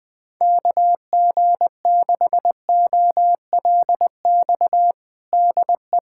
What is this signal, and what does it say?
Morse code, heard as a keyed tone, about 20 wpm, 700 Hz. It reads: KG6OLX DE
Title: USA Callsign DE Callsign - 6 Characters - 20wpm